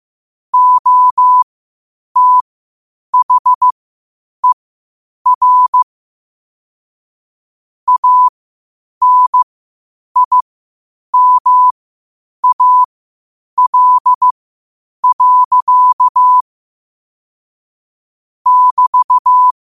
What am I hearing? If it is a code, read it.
Morse code, heard as a keyed tone, about 15 wpm, spaced as for 8 wpm, 1000 Hz. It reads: OTHER ANIMAL. <BT>
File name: qotdfiles / 8wpm